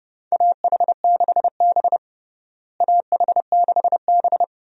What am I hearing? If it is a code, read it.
Morse code, heard as a keyed tone, about 32 wpm, 700 Hz. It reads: AH6B AH6B
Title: USA Call Signs - 4 Characters - Repeated 2x - 30wpm